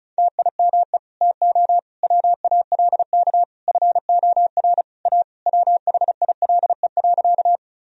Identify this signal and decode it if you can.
Morse code, heard as a keyed tone, about 36 wpm, 700 Hz. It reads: TIME TO WALK FOR A WHILE.
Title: Sentences from Top 1000 Words - Review Part 1 - 35wpm